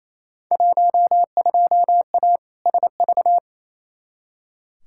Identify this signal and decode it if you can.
Morse code, heard as a keyed tone, about 28 wpm, 700 Hz. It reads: 12A SV
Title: Field Day Exchanges Encore - 28wpm